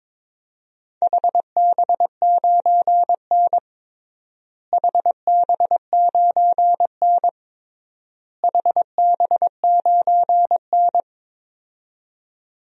Morse code, heard as a keyed tone, about 21 wpm, 700 Hz. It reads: HB9N HB9N HB9N
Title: DXCC-100 Call Signs - 4 Characters - Repeated 3x - 22wpm